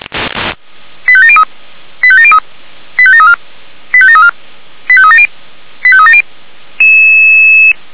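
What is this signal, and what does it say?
Five-tone selective call (ZVEI), heard as quick runs of five tones, tones of about 70 ms. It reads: 85703 85703 85734 85734 85370 85370
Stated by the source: ZVEI five-tone call 89494 received by SDR, recorded by Feuerwehr Orlamünde server.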